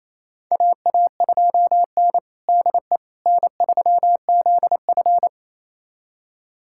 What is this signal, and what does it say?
Morse code, heard as a keyed tone, about 28 wpm, 700 Hz. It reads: AA2N DE N3ZF
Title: USA Callsign DE Callsign - 4 Characters - 28wpm